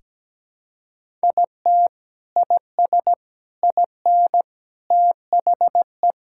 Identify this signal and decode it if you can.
Morse code, heard as a keyed tone, about 17 wpm, 700 Hz. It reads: IT IS IN THE
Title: Sentences from Top 1000 Words - Review Part 5 - 17wpm